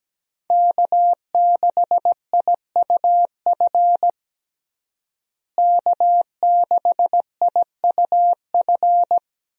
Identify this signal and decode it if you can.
Morse code, heard as a keyed tone, about 17 wpm, 700 Hz. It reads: K6IUF K6IUF
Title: USA Call Signs - 5 Characters - Repeated 2x - 17wpm